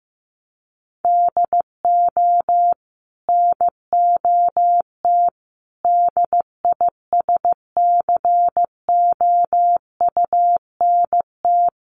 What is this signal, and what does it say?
Morse code, heard as a keyed tone, about 15 wpm, 700 Hz. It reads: DO NOT DISCOUNT
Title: Sentences from Top 1200 Words 15wpm